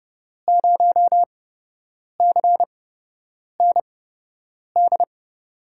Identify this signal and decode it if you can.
Morse code, heard as a keyed tone, about 32 wpm, spaced as for 8 wpm, 700 Hz. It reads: 0CND
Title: USA Call Signs - 6 Characters - 8x Character Spacing - 30wpm